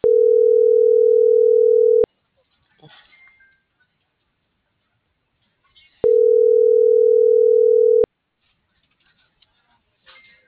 Ringback tone (North America), 440 + 480 Hz.